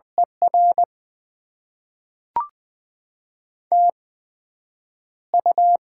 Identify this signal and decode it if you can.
Morse code, heard as a keyed tone, about 20 wpm, spaced as for 9 wpm, 700 Hz. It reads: ER TU
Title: Top 20 QSO Elements - 8x Character Spacing - 20wpm